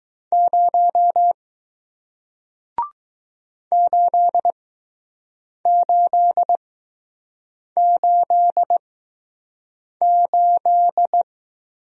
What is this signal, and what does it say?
Morse code, heard as a keyed tone, about 20 wpm, spaced as for 6 wpm, 700 Hz. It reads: 0 8888